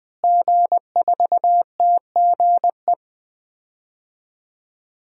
Morse code, heard as a keyed tone, about 21 wpm, 700 Hz. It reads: G4TGE
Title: DXCC-100 Call Signs - 5 Characters - Repeated 3x - 20wpm